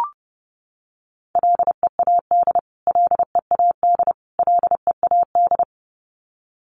Morse code, heard as a keyed tone, about 28 wpm, 700 Hz. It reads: LEAD LEAD LEAD
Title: Top 500 Words Repeated 30wpm